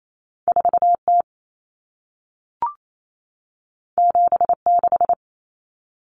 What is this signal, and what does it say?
Morse code, heard as a keyed tone, about 28 wpm, 700 Hz. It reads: 4T 76
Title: Common Cut Numbers - 2-Digit 28wpm